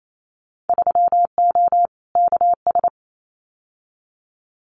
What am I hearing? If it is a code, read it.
Morse code, heard as a keyed tone, about 28 wpm, 700 Hz. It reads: 3O KS
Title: Winter Field Day Exchanges 28wpm